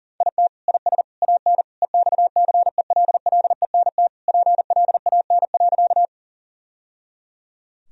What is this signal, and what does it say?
Morse code, heard as a keyed tone, about 40 wpm, 700 Hz. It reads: IT IS AN EXCELLENT PLAN.
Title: Sentences from Top 1000 Words - Review Part 2 - 40wpm